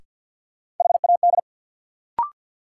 Morse code, heard as a keyed tone, about 49 wpm, 700 Hz. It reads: HAD
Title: Sets of 2 Words - New Word Drill - 50wpm